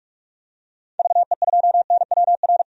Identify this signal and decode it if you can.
Morse code, heard as a keyed tone, about 45 wpm, 700 Hz. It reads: VE2NWR
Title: DXCC-100 Call Signs - 6 Characters - 45wpm